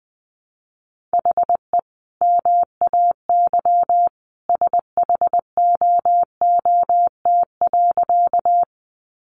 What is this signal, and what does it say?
Morse code, heard as a keyed tone, about 20 wpm, 700 Hz. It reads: HE MAY SHOOT.
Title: Sentences from Top 1200 Words 20wpm